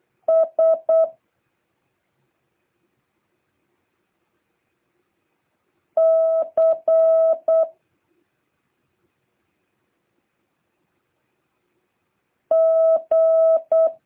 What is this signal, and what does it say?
Morse code, heard as a keyed tone, about 8 wpm, 650 Hz. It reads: S C G